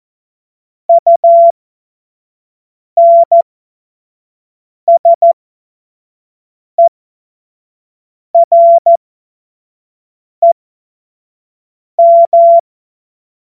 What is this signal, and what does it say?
Morse code, heard as a keyed tone, about 14 wpm, spaced as for 5 wpm, 680 Hz. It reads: UNSEREM
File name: Morsekurs / Witze / Witz 15. Februar 2022 25 F70